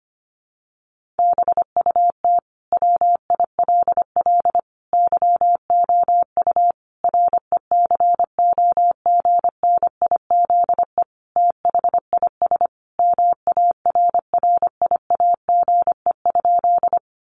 Morse code, heard as a keyed tone, about 25 wpm, 700 Hz. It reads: BUT WILL YOU RECOGNIZE THIS MARRIAGE?
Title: Sentences from Top 1400 Words 25wpm